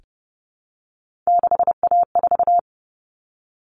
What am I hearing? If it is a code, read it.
Morse code, heard as a keyed tone, about 30 wpm, 700 Hz. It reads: 6A4
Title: All Cut Numbers - 3-Digit 30wpm